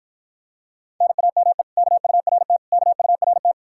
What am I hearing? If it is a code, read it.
Morse code, heard as a keyed tone, about 52 wpm, 700 Hz. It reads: NAME KURT KURT